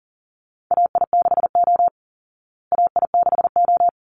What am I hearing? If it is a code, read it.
Morse code, heard as a keyed tone, about 40 wpm, 700 Hz. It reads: AI6O AI6O
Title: Call Signs Repeated 3x - 40wpm